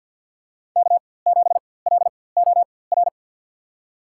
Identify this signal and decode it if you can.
Morse code, heard as a keyed tone, about 53 wpm, spaced as for 24 wpm, 700 Hz. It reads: K7LOR
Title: USA Call Signs - 5 Characters - 4x Character Spacing - 50wpm